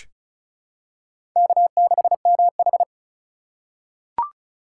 Morse code, heard as a keyed tone, about 36 wpm, 700 Hz. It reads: K6MH
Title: USA Call Signs - 4 Characters - 3x Character Spacing - 35wpm